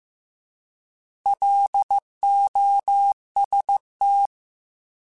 Morse code, heard as a keyed tone, about 15 wpm, 800 Hz. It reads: LOST
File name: Alphabetical / Alphabetical mp3s / Ls @13